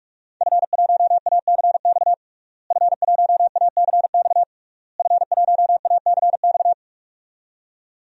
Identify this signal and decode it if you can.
Morse code, heard as a keyed tone, about 41 wpm, 700 Hz. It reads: F1ACX F1ACX F1ACX
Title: DXCC-100 Call Signs - 5 Characters - Repeated 3x - 45wpm